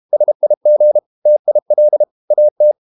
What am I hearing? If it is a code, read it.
Morse code, heard as a keyed tone, about 32 wpm, 600 Hz. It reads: SIG TIL AT
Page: Ferie 32 wpm